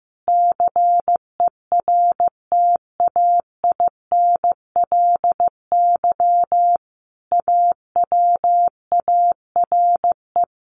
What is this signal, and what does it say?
Morse code, heard as a keyed tone, about 15 wpm, 700 Hz. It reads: CERTAINLY AWARE
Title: Sentences from Top 1200 Words 15wpm